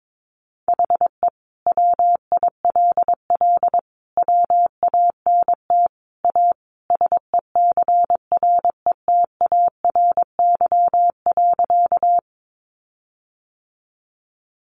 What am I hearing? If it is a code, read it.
Morse code, heard as a keyed tone, about 22 wpm, 700 Hz. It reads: HE WILL WANT A SECRETARY.
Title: Sentences from Top 1300 Words 22wpm